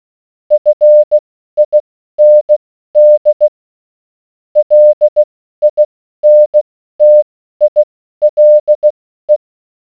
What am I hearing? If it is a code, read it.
Morse code, heard as a keyed tone, about 16 wpm, spaced as for 12 wpm, 600 Hz. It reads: FIND LINTILE